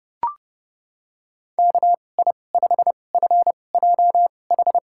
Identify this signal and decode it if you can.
Morse code, heard as a keyed tone, about 30 wpm, spaced as for 22 wpm, 700 Hz. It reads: KI5FJH
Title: USA Call Signs - 6 Characters - 2x Character Spacing - 30wpm